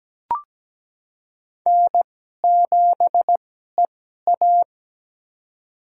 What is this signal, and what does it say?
Morse code, heard as a keyed tone, about 17 wpm, spaced as for 12 wpm, 700 Hz. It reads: N7EA
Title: USA Call Signs - 4 Characters - 2x Character Spacing - 17wpm